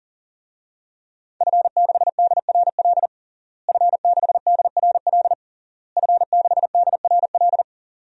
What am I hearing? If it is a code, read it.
Morse code, heard as a keyed tone, about 40 wpm, 700 Hz. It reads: F6DRL F6DRL F6DRL
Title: DXCC-100 Call Signs - 5 Characters - Repeated 3x - 40wpm